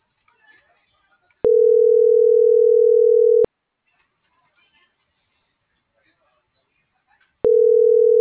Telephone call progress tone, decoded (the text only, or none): ringback tone